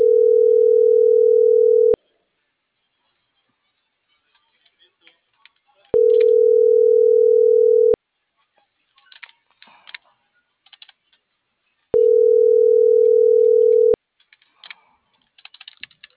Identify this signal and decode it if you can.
Ringback tone (North America), 440 + 480 Hz.